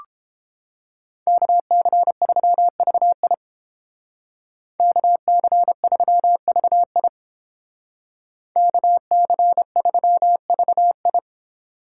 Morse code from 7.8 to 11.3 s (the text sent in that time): KC3VI